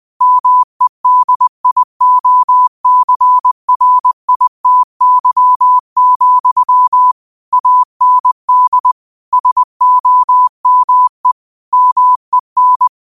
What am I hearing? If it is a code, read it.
Morse code, heard as a keyed tone, about 20 wpm, 1000 Hz. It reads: MEDIOCRITY, AND SOME MEN
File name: qotdfiles / 20wpm